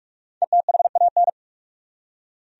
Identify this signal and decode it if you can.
Morse code, heard as a keyed tone, about 46 wpm, 700 Hz. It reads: ETHAN
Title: USA Common Names - 45wpm